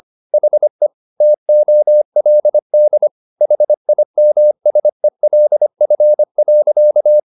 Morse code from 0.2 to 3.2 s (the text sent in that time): HE TOLD